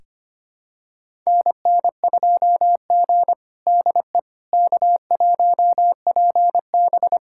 Morse code, heard as a keyed tone, about 25 wpm, 700 Hz. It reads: NN2G DE K1PB
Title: USA Callsign DE Callsign - 4 Characters - 25wpm